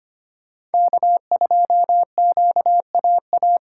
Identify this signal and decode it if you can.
Morse code, heard as a keyed tone, about 32 wpm, 700 Hz. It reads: K2QAA